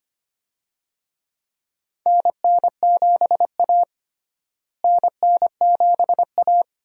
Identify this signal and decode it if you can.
Morse code, heard as a keyed tone, about 25 wpm, 700 Hz. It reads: NN7A NN7A
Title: USA Call Signs - 4 Characters - Repeated 2x - 25wpm